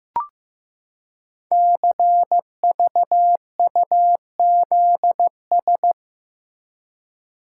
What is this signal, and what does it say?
Morse code, heard as a keyed tone, about 15 wpm, 700 Hz. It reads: CVUZS
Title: Code Groups - 5 Letters - 15wpm